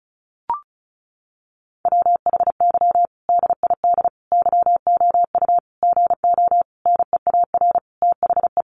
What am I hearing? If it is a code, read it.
Morse code, heard as a keyed tone, about 34 wpm, 700 Hz. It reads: WHY DID YOU GO NEAR THE
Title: Sentences from Top 1100 Words 35wpm